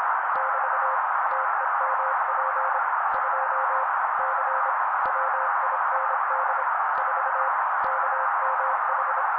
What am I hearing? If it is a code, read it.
Morse code, heard as a keyed tone, about 26 wpm, 530 Hz. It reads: <BT> TEMP 2 C WIND 4 KMH
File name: daily / 25wpm noise